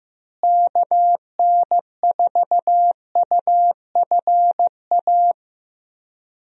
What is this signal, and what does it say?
Morse code, heard as a keyed tone, about 15 wpm, 700 Hz. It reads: KN4UFA
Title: USA Call Signs - 6 Characters - Repeated 3x - 15wpm